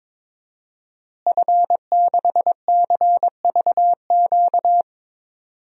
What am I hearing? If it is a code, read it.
Morse code, heard as a keyed tone, about 21 wpm, 700 Hz. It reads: F6CVQ
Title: DXCC-100 Call Signs - 5 Characters - Repeated 3x - 22wpm